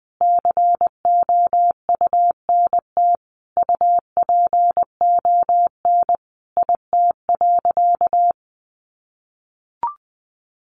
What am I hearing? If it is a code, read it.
Morse code, heard as a keyed tone, about 20 wpm, 700 Hz. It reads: COUNT UPON IT.
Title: Sentences from Top 1500 Words 20wpm